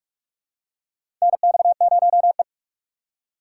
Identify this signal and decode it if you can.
Morse code, heard as a keyed tone, about 46 wpm, 700 Hz. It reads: NX0E
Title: USA Call Signs - 4 Characters - 6x Character Spacing - 45wpm